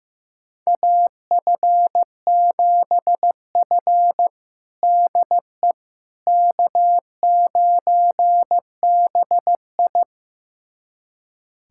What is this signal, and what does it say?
Morse code, heard as a keyed tone, about 15 wpm, 700 Hz. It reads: AF7F DE K9BI
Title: USA Callsign DE Callsign - 4 Characters - 15wpm